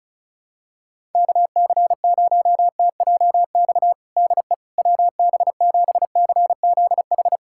Morse code, heard as a keyed tone, about 34 wpm, 700 Hz. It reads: KC0TJX DE WB7CZH
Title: USA Callsign DE Callsign 35wpm